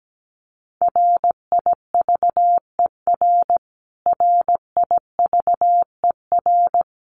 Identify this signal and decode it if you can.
Morse code, heard as a keyed tone, about 17 wpm, 700 Hz. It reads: RIVER RIVER
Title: Top 500 Words Repeated 17wpm